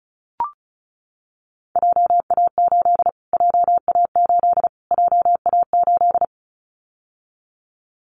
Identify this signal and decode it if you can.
Morse code, heard as a keyed tone, about 34 wpm, 700 Hz. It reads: JA8 JA8 JA8
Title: Callsign Prefixes for Top 25 DXCC - Repeated 3x - 35wpm